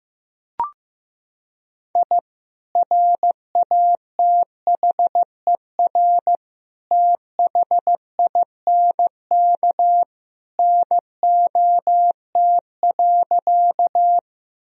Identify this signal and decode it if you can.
Morse code, heard as a keyed tone, about 15 wpm, 700 Hz. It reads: I RATHER THINK NOT.
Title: Sentences from Top 1000 Words - Review Part 4 - 15wpm